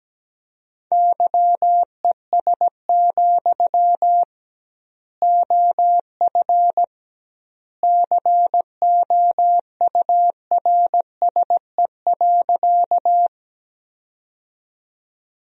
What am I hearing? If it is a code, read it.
Morse code, heard as a keyed tone, about 17 wpm, 700 Hz. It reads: YES, OF COURSE.
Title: 2x Word-Spacing - Sentences from Top 500 Words - 17wpm